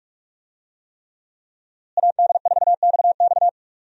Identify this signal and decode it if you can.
Morse code, heard as a keyed tone, about 42 wpm, 700 Hz. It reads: AD4XX
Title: USA Call Signs - 5 Characters - Repeated 3x - 45wpm